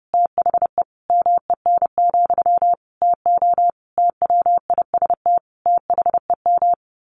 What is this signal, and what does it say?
Morse code, heard as a keyed tone, about 30 wpm, 700 Hz. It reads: THE MEN, TO TWIST THEM
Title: A Sound Of Thunder 30wpm